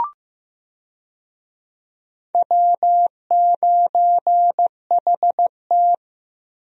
Morse code, heard as a keyed tone, about 15 wpm, 700 Hz. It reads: W9HT
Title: USA Call Signs - 4 Characters - Repeated 3x - 15wpm